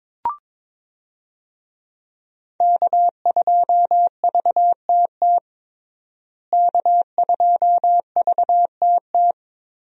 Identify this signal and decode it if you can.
Morse code, heard as a keyed tone, about 21 wpm, 700 Hz. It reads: K2VTT K2VTT
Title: USA Call Signs - 5 Characters - Repeated 2x - 22wpm